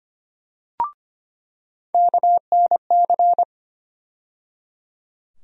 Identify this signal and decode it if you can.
Morse code, heard as a keyed tone, about 25 wpm, 700 Hz. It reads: KNC
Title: USA Call Sign Postfixes - 3 Letters 25wpm